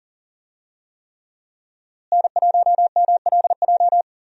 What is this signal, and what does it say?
Morse code, heard as a keyed tone, about 40 wpm, 700 Hz. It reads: N1MLJ
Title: USA Call Signs - 5 Characters - Repeated 3x - 40wpm